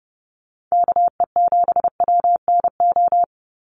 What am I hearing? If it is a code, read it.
Morse code, heard as a keyed tone, about 30 wpm, 700 Hz. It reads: KE7WNO
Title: USA Call Signs 30wpm